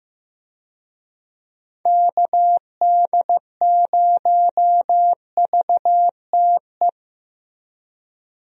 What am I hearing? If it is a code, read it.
Morse code, heard as a keyed tone, about 15 wpm, 700 Hz. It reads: KD0VTE
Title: USA Call Signs - 6 Characters - Repeated 3x - 15wpm